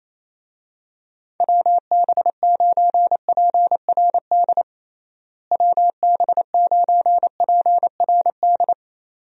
Morse code, heard as a keyed tone, about 28 wpm, 700 Hz. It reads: WB9PRD WB9PRD
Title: USA Call Signs - 6 Characters - Repeated 2x - 28wpm